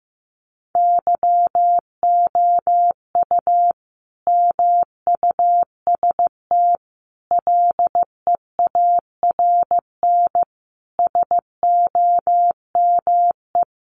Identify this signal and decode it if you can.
Morse code, heard as a keyed tone, about 14 wpm, 700 Hz. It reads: YOU MUST LEARN SOME
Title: Sentences from Top 900 Words 15wpm